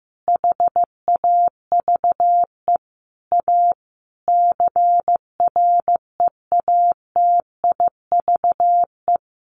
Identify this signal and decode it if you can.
Morse code, heard as a keyed tone, about 15 wpm, 700 Hz. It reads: HAVE A CREATIVE
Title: Sentences from Top 1300 Words 15wpm